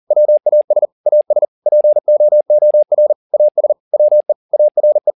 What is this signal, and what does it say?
Morse code, heard as a keyed tone, about 40 wpm, 600 Hz. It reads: WAS AS POOR AS WE ARE